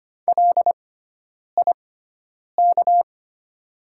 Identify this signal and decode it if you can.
Morse code, heard as a keyed tone, about 25 wpm, spaced as for 9 wpm, 690 Hz. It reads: LIK
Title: 6x Character Spacing - Top 100 Words - 25wpm